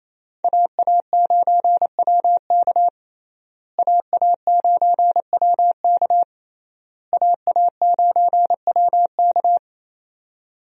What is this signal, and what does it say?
Morse code, heard as a keyed tone, about 28 wpm, 700 Hz. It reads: AA9WK AA9WK AA9WK
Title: USA Call Signs - 5 Characters - Repeated 3x - 28wpm